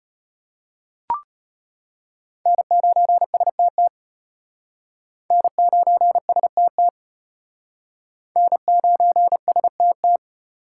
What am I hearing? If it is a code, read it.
Morse code, heard as a keyed tone, about 33 wpm, 700 Hz. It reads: N9STT N9STT N9STT